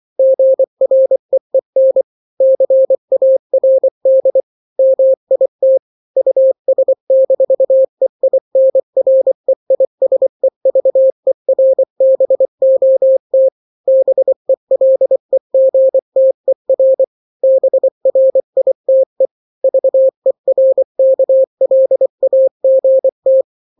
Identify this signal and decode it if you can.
Morse code, heard as a keyed tone, about 24 wpm, 530 Hz. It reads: GREEN CARD MIT US-EINREISEVERBOT BELEGTER BRITE VERKLAGT
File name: daily / 23wpm